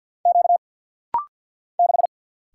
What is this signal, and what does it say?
Morse code, heard as a keyed tone, about 48 wpm, 700 Hz. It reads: Q 6
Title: Single Letter-Number - Rapid-Fire 50wpm